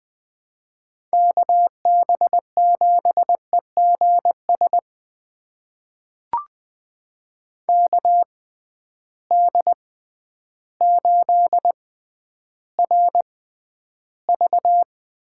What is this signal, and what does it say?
Morse code, heard as a keyed tone, about 20 wpm, spaced as for 11 wpm, 700 Hz. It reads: KB7EGS KD8RV
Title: USA Call Signs - 6 Characters - 6x Character Spacing - 20wpm